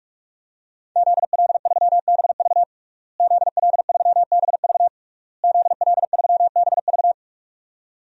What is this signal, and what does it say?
Morse code, heard as a keyed tone, about 46 wpm, 700 Hz. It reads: ZL3BV ZL3BV ZL3BV
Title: DXCC-100 Call Signs - 5 Characters - Repeated 3x - 45wpm